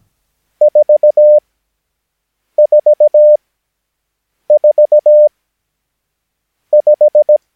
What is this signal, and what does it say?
Morse code, heard as a keyed tone, about 17 wpm, spaced as for 6 wpm, 600 Hz. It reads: 444 5
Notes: Övningsmaterial med alfabetet A till Ö samt siffror 0 till 9 enligt International/Continental Morse Code.